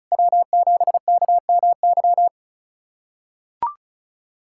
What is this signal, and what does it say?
Morse code, heard as a keyed tone, about 34 wpm, 700 Hz. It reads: W7KMY
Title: USA Callsign DE Callsign - 5 Characters - 35wpm